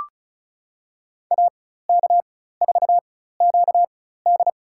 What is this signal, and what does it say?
Morse code, heard as a keyed tone, about 36 wpm, spaced as for 16 wpm, 700 Hz. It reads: AK4QD
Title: USA Call Signs - 5 Characters - 4x Character Spacing - 35wpm